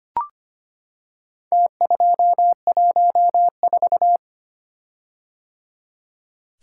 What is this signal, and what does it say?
Morse code, heard as a keyed tone, about 25 wpm, 700 Hz. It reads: T214
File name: practice / 25wpm / Four Letter-Number Combinations - 25wpm